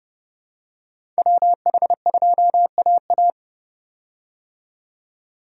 Morse code, heard as a keyed tone, about 30 wpm, 700 Hz. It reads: WH2AA
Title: USA Call Signs - 5 Characters - 30wpm